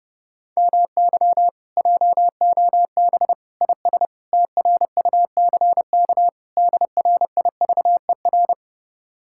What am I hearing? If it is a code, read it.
Morse code, heard as a keyed tone, about 30 wpm, 700 Hz. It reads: MY JOB IS TRUCK DRIVER